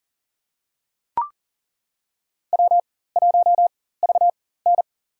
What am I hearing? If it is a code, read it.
Morse code, heard as a keyed tone, about 40 wpm, spaced as for 19 wpm, 700 Hz. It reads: W1VN
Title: USA Call Signs - 4 Characters - 4x Character Spacing - 40wpm